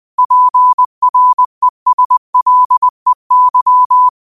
Morse code, heard as a keyed tone, about 20 wpm, 1000 Hz. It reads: PRESLEY